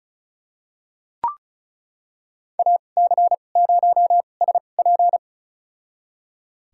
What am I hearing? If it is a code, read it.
Morse code, heard as a keyed tone, about 34 wpm, spaced as for 25 wpm, 700 Hz. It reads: AC0SP
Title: USA Call Signs - 5 Characters - 2x Character Spacing - 35wpm